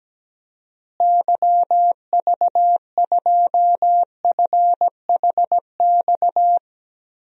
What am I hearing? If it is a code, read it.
Morse code, heard as a keyed tone, about 17 wpm, 700 Hz. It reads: YV2FHX
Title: DXCC-100 Call Signs - 6 Characters - Repeated 2x - 17wpm